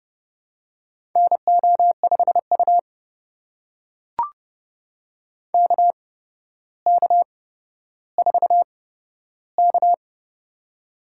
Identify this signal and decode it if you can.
Morse code, heard as a keyed tone, about 30 wpm, spaced as for 13 wpm, 700 Hz. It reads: NO5U KK4K
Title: USA Call Signs - 4 Characters - 8x Character Spacing - 30wpm